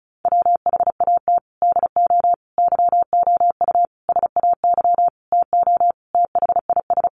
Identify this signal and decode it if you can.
Morse code, heard as a keyed tone, about 34 wpm, 700 Hz. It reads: WHAT DO YOU SAY TO THIS
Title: Sentences from Top 1400 Words 35wpm